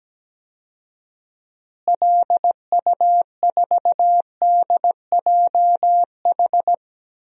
Morse code, heard as a keyed tone, about 17 wpm, 700 Hz. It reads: LU4DJH
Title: DXCC-100 Call Signs - 6 Characters - Repeated 2x - 17wpm